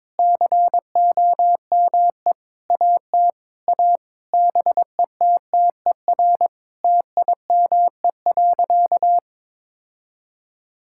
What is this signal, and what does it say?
Morse code, heard as a keyed tone, about 22 wpm, 700 Hz. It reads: COME AT A BETTER TIME.
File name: practice / 22wpm / Sentences with 8-Word Phrases - 22wpm